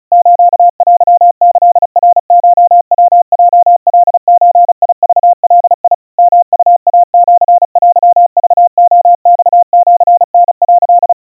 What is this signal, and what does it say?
Morse code (Japanese Wabun code), heard as a keyed tone, about 36 wpm, 700 Hz. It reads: ステキナコヤヲカゾクガ ヨウイシテクレマシタ」
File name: kawa / Morse sample / kamyu 130 cpm 700 Hz Crispy